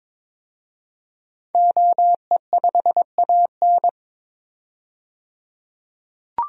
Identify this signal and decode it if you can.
Morse code, heard as a keyed tone, about 22 wpm, 700 Hz. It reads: OE5AN
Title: DXCC-100 Call Signs - 5 Characters - 22wpm